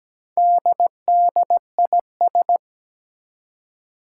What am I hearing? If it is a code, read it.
Morse code, heard as a keyed tone, about 17 wpm, 700 Hz. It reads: DDIS
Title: Four Challenging Letter-Number Combinations 17wpm